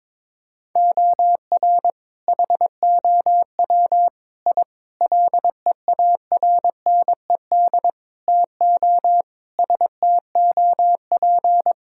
Morse code, heard as a keyed tone, about 22 wpm, 700 Hz. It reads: OR HOW I LEARNED TO STOP
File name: practice / 22wpm / Sentences with 8-Word Phrases - 22wpm